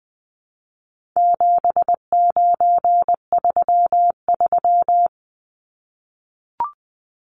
Morse code, heard as a keyed tone, about 20 wpm, 700 Hz. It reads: 7933
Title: Numbers - 4-Digit 20wpm